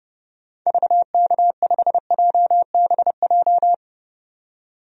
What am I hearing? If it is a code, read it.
Morse code, heard as a keyed tone, about 29 wpm, 700 Hz. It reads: VK5JBJ
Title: DXCC-100 Call Signs - 6 Characters - 30wpm